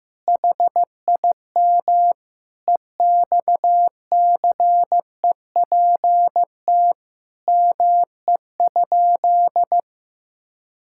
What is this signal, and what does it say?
Morse code, heard as a keyed tone, about 15 wpm, 700 Hz. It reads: HIM EXCEPT ME?
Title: Sentences from Top 1600 Words 15wpm